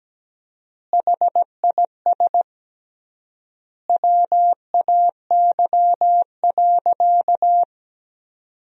Morse code, heard as a keyed tone, about 17 wpm, 700 Hz. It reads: HIS WAY.
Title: 3x Word-Spacing - Sentences from Top 300 Words - 17wpm